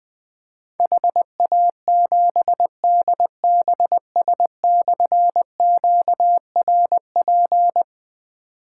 Morse code, heard as a keyed tone, about 21 wpm, 700 Hz. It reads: HA7DBS/QRP
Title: Difficult Call Signs 20wpm